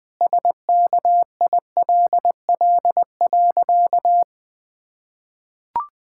Morse code, heard as a keyed tone, about 19 wpm, 700 Hz. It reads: SKILL.
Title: Sentences with 3-Word Phrases - 20wpm